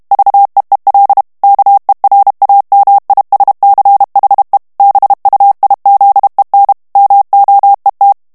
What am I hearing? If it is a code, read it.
Morse code, heard as a keyed tone, about 32 wpm, 800 Hz. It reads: VEEL KERAMISCHE BUIZEN MOET
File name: HOLLANDAIS texte clair 2 vitesse 30 mots minute